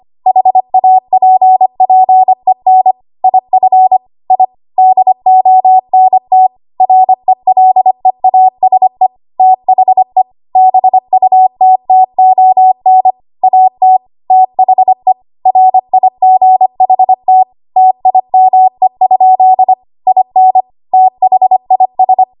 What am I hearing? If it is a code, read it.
Morse code, heard as a keyed tone, about 25 wpm, 750 Hz. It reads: HAPPEN IF I DONT RELEASE THE BUTTON AT THE RIGHT TIME? IN THIS
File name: Archive / 25 WPM / 25WPM